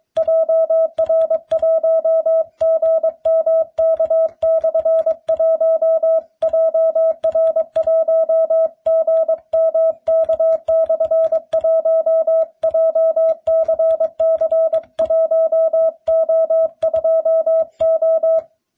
Morse code, heard as a keyed tone, about 22 wpm, 650 Hz. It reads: JR1GMK/1 JR1GMK/1 JCC1O2O